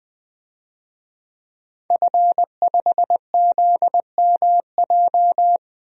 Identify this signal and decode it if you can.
Morse code, heard as a keyed tone, about 20 wpm, 700 Hz. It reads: F5ZMJ
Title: DXCC-100 Call Signs - 5 Characters - Repeated 2x - 20wpm